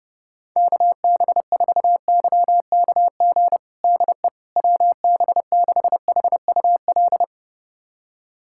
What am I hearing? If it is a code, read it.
Morse code, heard as a keyed tone, about 30 wpm, 700 Hz. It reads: KB4YKG DE WB6HUL
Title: USA Callsign DE Callsign 30wpm